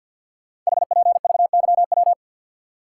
Morse code, heard as a keyed tone, about 50 wpm, 700 Hz. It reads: HPVYW
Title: Code Groups - 5 Letters - 50wpm